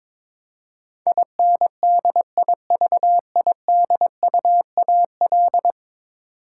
Morse code, heard as a keyed tone, about 22 wpm, 700 Hz. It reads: INDIVIDUAL
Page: morse-code-ninja-practice-22wpm